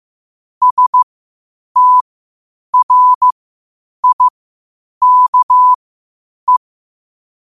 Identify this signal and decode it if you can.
Morse code, heard as a keyed tone, about 15 wpm, spaced as for 8 wpm, 1000 Hz. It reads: STRIKE
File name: qotdfiles / 8wpm